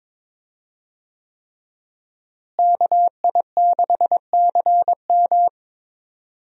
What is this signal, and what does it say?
Morse code, heard as a keyed tone, about 21 wpm, 700 Hz. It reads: KI6CM
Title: USA Call Signs - 5 Characters - Repeated 3x - 22wpm